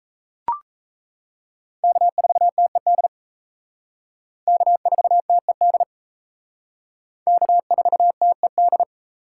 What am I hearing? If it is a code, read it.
Morse code, heard as a keyed tone, about 34 wpm, 700 Hz. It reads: K4TED K4TED K4TED
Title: Speed Racing - USA 1x3 Call Signs 28wpm